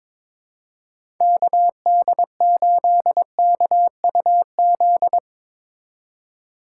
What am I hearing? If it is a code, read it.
Morse code, heard as a keyed tone, about 22 wpm, 700 Hz. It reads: KD8KUZ
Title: USA Call Signs - 6 Characters - 4x Character Spacing - 22wpm